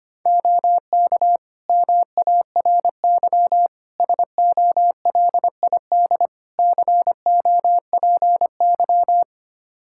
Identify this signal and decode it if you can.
Morse code, heard as a keyed tone, about 25 wpm, 700 Hz. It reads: OK MARY SOLID COPY
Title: Practice QSO - Exchange 3 - More Info - 25wpm